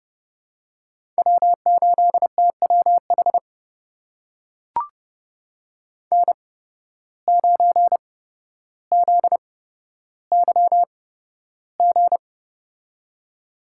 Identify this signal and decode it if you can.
Morse code, heard as a keyed tone, about 30 wpm, spaced as for 13 wpm, 700 Hz. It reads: W8TWH N9ZYG
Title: USA Call Signs - 5 Characters - 8x Character Spacing - 30wpm